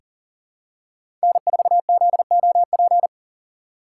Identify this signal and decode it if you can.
Morse code, heard as a keyed tone, about 37 wpm, 700 Hz. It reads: N4ZOP
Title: USA Call Signs - 6x Character Spacing - 40wpm